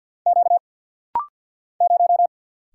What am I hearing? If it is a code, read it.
Morse code, heard as a keyed tone, about 50 wpm, 700 Hz. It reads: Q 0
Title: Single Letter-Number Alt - Rapid-Fire 50wpm